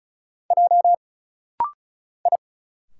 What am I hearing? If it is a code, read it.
Morse code, heard as a keyed tone, about 34 wpm, 700 Hz. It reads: J I
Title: Single Letter-Number Alt - Rapid-Fire 35wpm